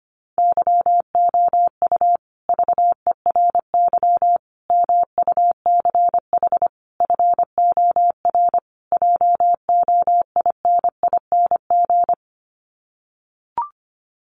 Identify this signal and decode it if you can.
Morse code, heard as a keyed tone, about 25 wpm, 700 Hz. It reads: YOU VERY MUCH FOR JOINING
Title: Sets of 5 Words Encore 25wpm